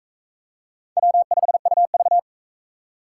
Morse code, heard as a keyed tone, about 41 wpm, 700 Hz. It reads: W5UV